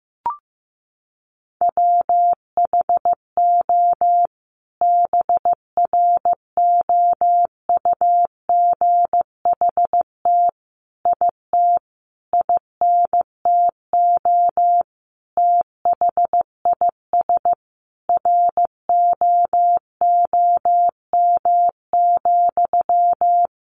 Morse code, heard as a keyed tone, about 15 wpm, 700 Hz. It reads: WHO BROUGHT IT INTO THIS ROOM,
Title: Sentences from Top 600 Words 15wpm